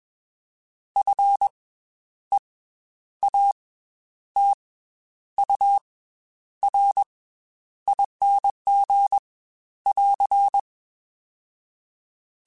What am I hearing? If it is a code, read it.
Morse code, heard as a keyed tone, about 21 wpm, 800 Hz. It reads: F E A T U R ING <AR>
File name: Special INGs / Special INGs-mp3s / Slow ing @20